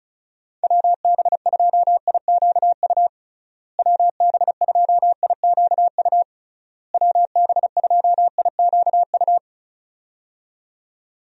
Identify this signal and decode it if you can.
Morse code, heard as a keyed tone, about 36 wpm, 700 Hz. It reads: WB2IQU WB2IQU WB2IQU
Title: USA Call Signs - 6 Characters - Repeated 3x - 35wpm